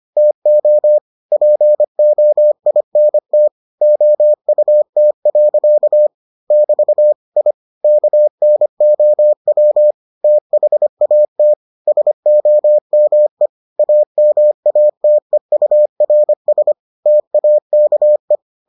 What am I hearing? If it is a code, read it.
Morse code, heard as a keyed tone, about 25 wpm, 600 Hz. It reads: TO POINT OUT. <BT> I KNOW THAT SOME AMATEURS TAKE